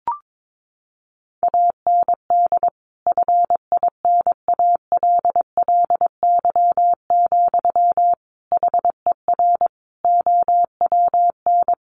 Morse code, heard as a keyed tone, about 22 wpm, 700 Hz. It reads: AND FINALLY, HER OWN
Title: Sentences from Top 1100 Words 22wpm